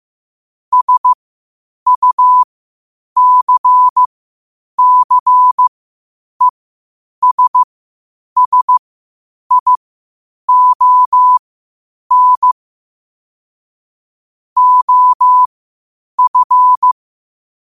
Morse code, heard as a keyed tone, about 15 wpm, spaced as for 8 wpm, 1000 Hz. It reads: SUCCESSION OF
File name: qotdfiles / 8wpm